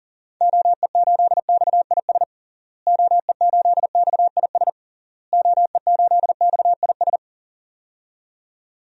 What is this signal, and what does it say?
Morse code, heard as a keyed tone, about 40 wpm, 700 Hz. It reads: OE8XIS OE8XIS OE8XIS
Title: DXCC-100 Call Signs - 6 Characters - Repeated 3x - 40wpm